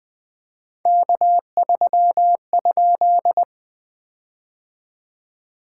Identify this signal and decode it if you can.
Morse code, heard as a keyed tone, about 20 wpm, 700 Hz. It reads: K3?
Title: POTA QSO Practice - 20wpm